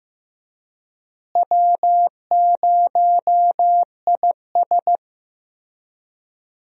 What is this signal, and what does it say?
Morse code, heard as a keyed tone, about 15 wpm, 700 Hz. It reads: W0IS
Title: USA Call Signs - 4 Characters - Repeated 3x - 15wpm